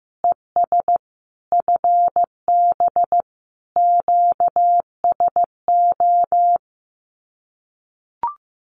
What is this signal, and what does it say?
Morse code, heard as a keyed tone, about 15 wpm, 700 Hz. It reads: ES FB QSO
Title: Common QSO Phrases Encore 15wpm